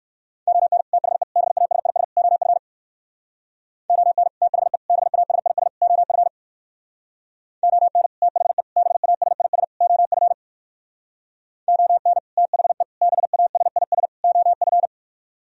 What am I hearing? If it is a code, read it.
Morse code, heard as a keyed tone, about 56 wpm, 700 Hz. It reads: ON THE BASIS OF ON THE BASIS OF ON THE BASIS OF ON THE BASIS OF
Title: Speed-Racing - Sets of 4 Words 45wpm